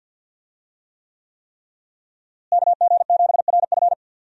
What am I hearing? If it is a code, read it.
Morse code, heard as a keyed tone, about 48 wpm, 700 Hz. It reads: KG7RF
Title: USA Call Signs - 5 Characters - Repeated 3x - 50wpm